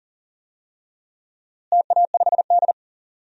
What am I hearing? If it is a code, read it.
Morse code, heard as a keyed tone, about 40 wpm, 700 Hz. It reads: TA5D
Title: DXCC-100 Call Signs - 4 Characters - Repeated 2x - 40wpm